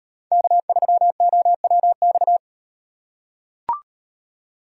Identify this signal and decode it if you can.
Morse code, heard as a keyed tone, about 36 wpm, 700 Hz. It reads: K3OWX